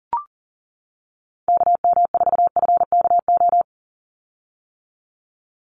Morse code, heard as a keyed tone, about 40 wpm, 700 Hz. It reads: KM4FKO
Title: USA Call Signs 40wpm